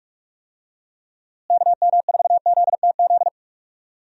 Morse code, heard as a keyed tone, about 46 wpm, 700 Hz. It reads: KM4ZTZ
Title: USA Call Signs - 6 Characters - Repeated 3x - 45wpm